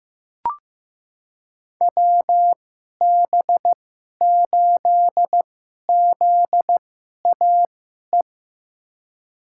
Morse code, heard as a keyed tone, about 15 wpm, spaced as for 11 wpm, 700 Hz. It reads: WB8ZAE
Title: USA Call Signs - 6 Characters - 2x Character Spacing - 15wpm